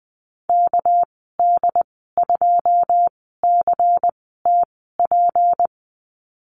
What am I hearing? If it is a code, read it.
Morse code, heard as a keyed tone, about 20 wpm, spaced as for 15 wpm, 700 Hz. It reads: KD2CTP
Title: 207 Lesson 33g - Letter j - Call Signs 20wpm